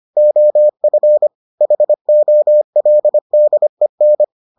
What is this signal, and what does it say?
Morse code, heard as a keyed tone, about 25 wpm, 600 Hz. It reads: OF HOLDEN